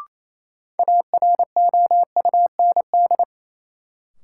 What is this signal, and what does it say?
Morse code, heard as a keyed tone, about 28 wpm, 700 Hz. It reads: AROUND
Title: Sets of 2 Words - New Word Drill - 28wpm